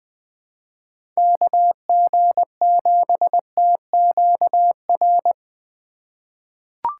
Morse code, heard as a keyed tone, about 20 wpm, 700 Hz. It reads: KG7TQR
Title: USA Call Signs - 6 Characters - 4x Character Spacing - 20wpm